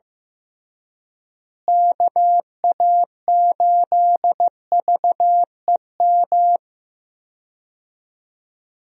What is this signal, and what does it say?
Morse code, heard as a keyed tone, about 15 wpm, 700 Hz. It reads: KA8VEM
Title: USA Call Signs - 6 Characters - Repeated 2x - 15wpm